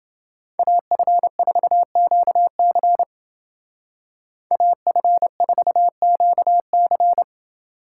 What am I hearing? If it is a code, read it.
Morse code, heard as a keyed tone, about 31 wpm, 700 Hz. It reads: AF4QC AF4QC